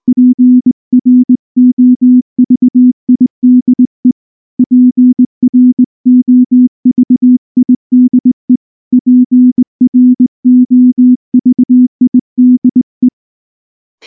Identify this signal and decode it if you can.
Morse code, heard as a keyed tone, about 19 wpm, 250 Hz. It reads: PROVIDE PROVIDE PROVIDE
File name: morse / morse-training-20WPM-2